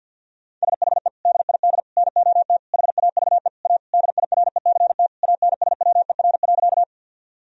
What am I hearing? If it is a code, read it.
Morse code, heard as a keyed tone, about 51 wpm, 700 Hz. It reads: SHE DID NOT HAVE A DIRECT ANSWER.
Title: Sentences from Top 1000 Words - Review Part 2 - 50wpm